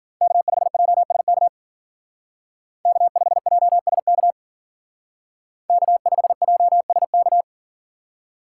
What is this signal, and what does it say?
Morse code, heard as a keyed tone, about 51 wpm, 700 Hz. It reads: K5JSK K5JSK K5JSK